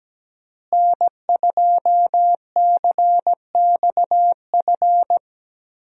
Morse code, heard as a keyed tone, about 17 wpm, 700 Hz. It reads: N2CXF
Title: USA Call Signs - 5 Characters - Repeated 3x - 17wpm